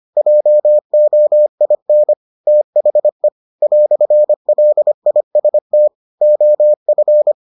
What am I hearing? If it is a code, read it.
Morse code, heard as a keyed tone, about 25 wpm, 600 Hz. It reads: JOIN THE "LIST OF